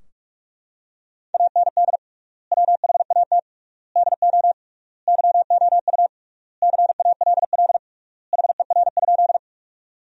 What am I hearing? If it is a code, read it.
Morse code, heard as a keyed tone, about 41 wpm, 700 Hz. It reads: AND WHAT DO YOU CALL HER?
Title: 3x Word-Spacing - Sentences from Top 100 Words - 45wpm